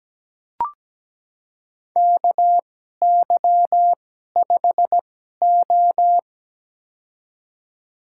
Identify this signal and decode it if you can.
Morse code, heard as a keyed tone, about 17 wpm, spaced as for 12 wpm, 700 Hz. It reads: KY5O
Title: USA Call Signs - 4 Characters - 2x Character Spacing - 17wpm